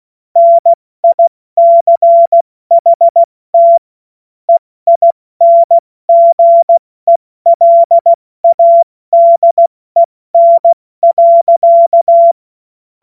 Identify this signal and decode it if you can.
Morse code, heard as a keyed tone, about 16 wpm, 680 Hz. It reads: NICHT EINGELADEN.
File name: Morsekurs / Witze / Witz 08. Februar 2022 70 F80